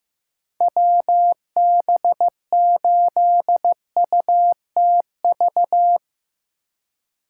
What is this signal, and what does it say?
Morse code, heard as a keyed tone, about 15 wpm, 700 Hz. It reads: WB8UTV
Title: USA Call Signs - 4x Character Spacing - 15wpm